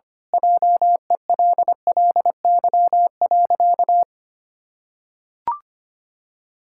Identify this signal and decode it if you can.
Morse code, heard as a keyed tone, about 25 wpm, 700 Hz. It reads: JELLY.
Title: Pangram Challenge v2 - 25wpm